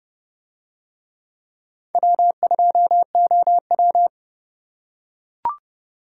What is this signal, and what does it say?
Morse code, heard as a keyed tone, about 29 wpm, 700 Hz. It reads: W2OW